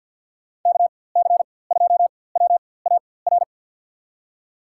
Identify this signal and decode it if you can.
Morse code, heard as a keyed tone, about 52 wpm, spaced as for 23 wpm, 700 Hz. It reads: KC2WAR
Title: USA Call Signs - 4x Character Spacing - 50wpm